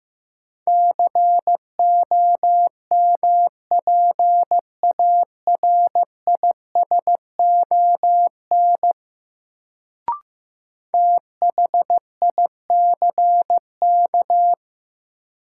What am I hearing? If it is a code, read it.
Morse code, heard as a keyed tone, about 15 wpm, 700 Hz. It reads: COMPARISON THICK
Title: Sentences from Top 1600 Words - New Word Drill - 15wpm